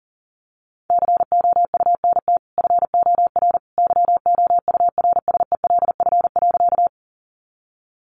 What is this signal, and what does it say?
Morse code, heard as a keyed tone, about 40 wpm, 700 Hz. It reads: COUNT FOR YOURSELF.
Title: Sentences from Top 1500 Words 40wpm